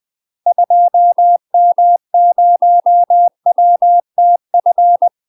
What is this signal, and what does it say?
Morse code, heard as a keyed tone, about 20 wpm, 700 Hz. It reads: 2M0WTF
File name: UK MASTER QSOs TRAINING/20WPM